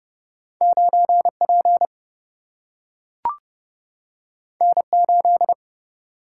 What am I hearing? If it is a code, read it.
Morse code, heard as a keyed tone, about 30 wpm, 700 Hz. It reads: 9 P N 8
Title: Single-Letter Single-Number 30wpm